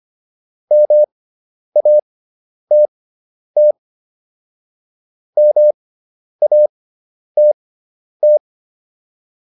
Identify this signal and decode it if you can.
Morse code, heard as a keyed tone, about 25 wpm, spaced as for 10 wpm, 600 Hz. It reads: MATT MATT
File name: QSO309 10